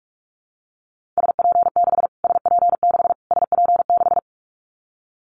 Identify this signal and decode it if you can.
Morse code, heard as a keyed tone, about 44 wpm, 700 Hz. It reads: SP6 SP6 SP6
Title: Callsign Prefixes for Top 25 DXCC - Repeated 3x - 45wpm